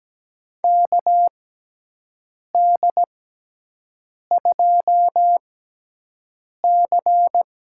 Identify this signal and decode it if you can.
Morse code, heard as a keyed tone, about 17 wpm, spaced as for 6 wpm, 700 Hz. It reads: KD2C
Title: USA Call Signs - 4 Characters - 6x Character Spacing - 17wpm